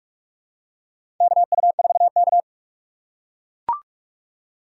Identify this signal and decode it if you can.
Morse code, heard as a keyed tone, about 44 wpm, 700 Hz. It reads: KU4K
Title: USA Call Signs - 6x Character Spacing - 45wpm